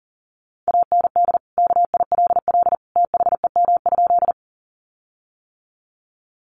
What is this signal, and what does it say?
Morse code, heard as a keyed tone, about 37 wpm, 700 Hz. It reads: AND KILL THEM?
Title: Sentences from Top 1400 Words 40wpm